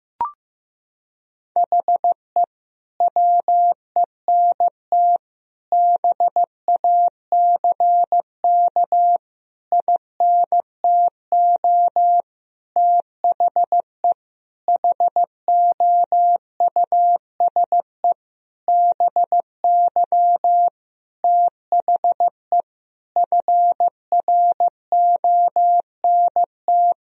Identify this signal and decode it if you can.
Morse code, heard as a keyed tone, about 15 wpm, 700 Hz. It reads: HE WENT BACK INTO THE HOUSE BY THE FRONT
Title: Sentences from Top 1000 Words - Review Part 2 - 15wpm